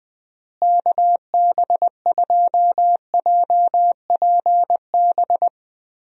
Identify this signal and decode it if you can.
Morse code, heard as a keyed tone, about 21 wpm, 700 Hz. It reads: KB2JPB
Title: USA Call Signs - 6 Characters - Repeated 3x - 20wpm